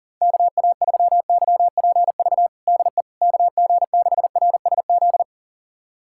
Morse code, heard as a keyed tone, about 40 wpm, 700 Hz. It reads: KA3YPV DE KG6RSZ
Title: USA Callsign DE Callsign 40wpm